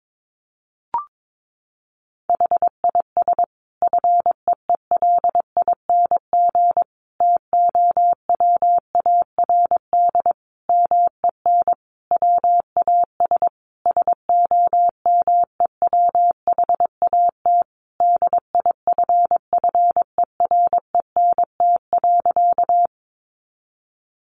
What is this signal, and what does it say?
Morse code, heard as a keyed tone, about 22 wpm, 700 Hz. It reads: HIS FEELING TOWARD MEN WAS SOMEWHAT DIFFERENT.
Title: Sentences from Top 1300 Words 22wpm